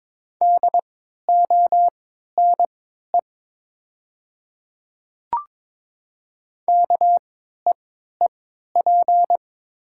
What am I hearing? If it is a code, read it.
Morse code, heard as a keyed tone, about 22 wpm, spaced as for 12 wpm, 700 Hz. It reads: DONE KEEP
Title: 3x Character Spacing - Top 500 Words - 22wpm